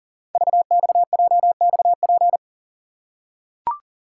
Morse code, heard as a keyed tone, about 40 wpm, 700 Hz. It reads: VXJXP
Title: Code Groups - 5 Letter:Numbers - 40wpm